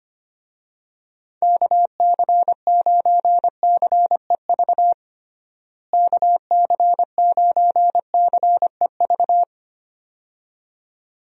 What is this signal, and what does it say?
Morse code, heard as a keyed tone, about 25 wpm, 700 Hz. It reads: KC9CEV KC9CEV
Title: USA Call Signs - 6 Characters - Repeated 2x - 25wpm